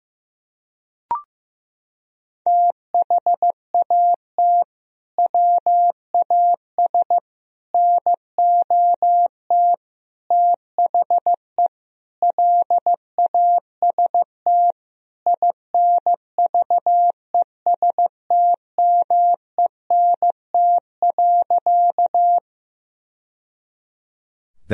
Morse code, heard as a keyed tone, about 15 wpm, 700 Hz. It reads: THAT WAS NOT THE LAST INVESTMENT.
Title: Sentences from Top 1000 Words - Review Part 4 - 15wpm